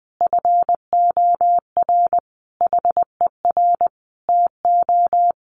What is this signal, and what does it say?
Morse code, heard as a keyed tone, about 20 wpm, 700 Hz. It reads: FOR HER TO
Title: Sentences from Top 1500 Words 20wpm